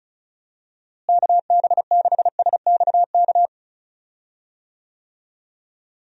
Morse code, heard as a keyed tone, about 34 wpm, 700 Hz. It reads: KB6SXK
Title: USA Call Signs - 6 Characters - Repeated 2x - 35wpm